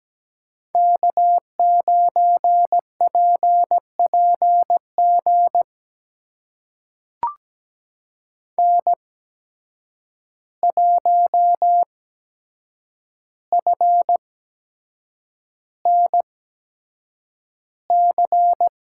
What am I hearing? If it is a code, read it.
Morse code, heard as a keyed tone, about 17 wpm, spaced as for 7 wpm, 700 Hz. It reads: K9PPG N1FNC
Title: USA Call Signs - 5 Characters - 8x Character Spacing - 17wpm